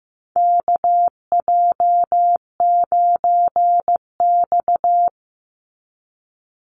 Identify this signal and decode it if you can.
Morse code, heard as a keyed tone, about 15 wpm, 700 Hz. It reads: KJ9X
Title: NA SOTA Chaser Callsigns 15wpm